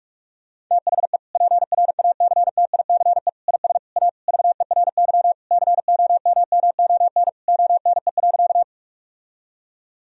Morse code, heard as a keyed tone, about 46 wpm, 700 Hz. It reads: THE PRACTICE IS A VERY COMMON ONE.
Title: Sentences from Top 1000 Words - Review Part 4 - 45wpm